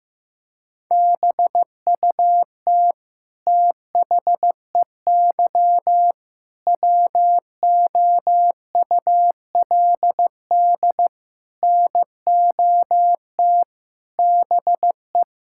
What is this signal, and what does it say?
Morse code, heard as a keyed tone, about 15 wpm, 700 Hz. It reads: BUT THEY WOULD NOT BE
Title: Sentences from Top 1000 Words - Review Part 5 - 15wpm